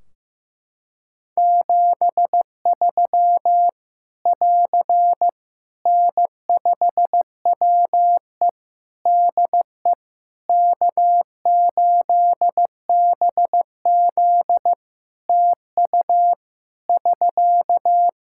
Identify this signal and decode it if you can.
Morse code, heard as a keyed tone, about 15 wpm, 700 Hz. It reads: 73 <AR> N5WE DE K8BZ TU <SK>
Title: K1USN SST QSO Practice 15wpm